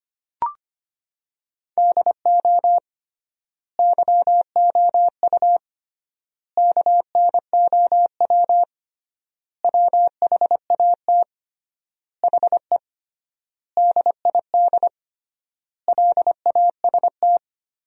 Morse code, heard as a keyed tone, about 25 wpm, 700 Hz. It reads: DO YOU KNOW WHAT HE DID LAST
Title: 3x Word-Spacing - Sentences from Top 300 Words - 25wpm